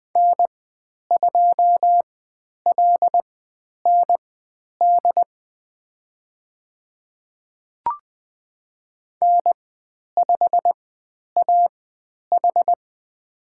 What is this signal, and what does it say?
Morse code, heard as a keyed tone, about 20 wpm, spaced as for 10 wpm, 700 Hz. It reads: N2LND N5AH
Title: 095 Lesson 16g - Letter c - Call Signs 20wpm